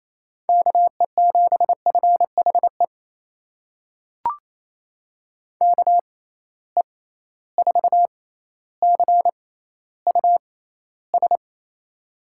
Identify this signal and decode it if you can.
Morse code, heard as a keyed tone, about 28 wpm, spaced as for 14 wpm, 700 Hz. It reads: KE7FHE KE4CUS
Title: USA Call Signs - 6 Characters - 6x Character Spacing - 28wpm